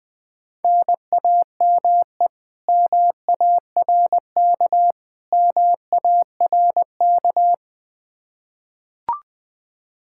Morse code, heard as a keyed tone, about 20 wpm, 700 Hz. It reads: NAME MARK MARK
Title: Practice QSO - Exchange 1 - Beginning - 20wpm